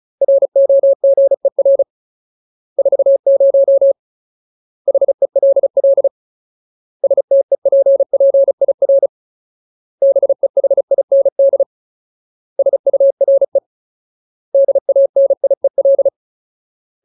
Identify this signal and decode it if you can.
Morse code, heard as a keyed tone, about 34 wpm, 550 Hz. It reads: ROGER 40 HELL STEPPIR BEHIND SURE DANIEL
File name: Ultimate Word List / 35WPM Ultimate